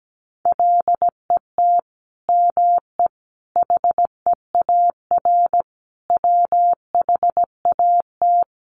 Morse code, heard as a keyed tone, about 17 wpm, 700 Hz. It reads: LET ME HEAR WHAT
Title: Sentences from Top 300 Words 17wpm